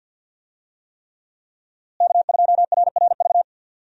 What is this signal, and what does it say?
Morse code, heard as a keyed tone, about 52 wpm, 700 Hz. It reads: K2RRV
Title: USA Call Signs - 5 Characters - Repeated 3x - 50wpm